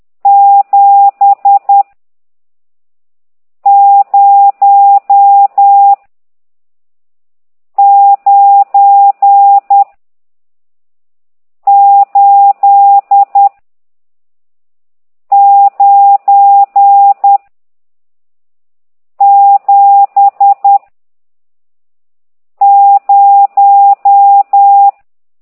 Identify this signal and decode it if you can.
Morse code, heard as a keyed tone, about 10 wpm, 800 Hz. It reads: 7 0 9 8 9 7 0
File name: ct2idw / exerc14